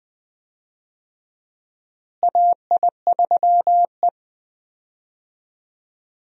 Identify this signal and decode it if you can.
Morse code, heard as a keyed tone, about 21 wpm, 700 Hz. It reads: AI3E
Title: USA Call Signs - 4 Characters - Repeated 3x - 20wpm